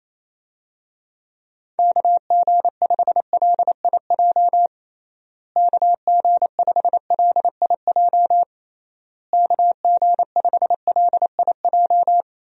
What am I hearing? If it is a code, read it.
Morse code, heard as a keyed tone, about 28 wpm, 700 Hz. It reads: KG5LIJ KG5LIJ KG5LIJ
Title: USA Call Signs - 6 Characters - Repeated 3x - 28wpm